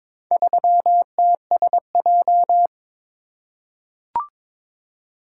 Morse code, heard as a keyed tone, about 22 wpm, 700 Hz. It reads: 3TSJ
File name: practice / 22wpm / Four Letter-Number Combinations - 22wpm